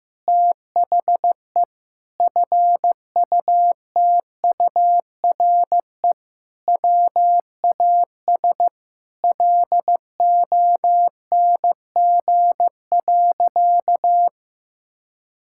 Morse code, heard as a keyed tone, about 16 wpm, 700 Hz. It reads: THE FUTURE WAS LONG.
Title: Sentences from Top 1000 Words - Review Part 6 - 15wpm